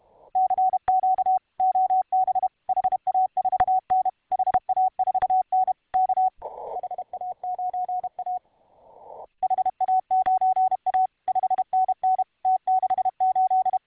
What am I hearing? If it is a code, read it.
Morse code, heard as a keyed tone, about 32 wpm, 750 Hz. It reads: CQ OB HA4N HA4N K HA9A HA9A 5NN T68